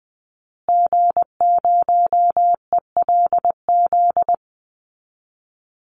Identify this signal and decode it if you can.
Morse code, heard as a keyed tone, about 19 wpm, 700 Hz. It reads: G0ELZ
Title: Call Signs Repeated 3x - 20wpm